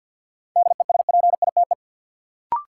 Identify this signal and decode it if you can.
Morse code, heard as a keyed tone, about 53 wpm, 700 Hz. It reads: DESPITE
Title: Sentences from Top 1000 Words - Review Part 7 - New Word Drill - 50wpm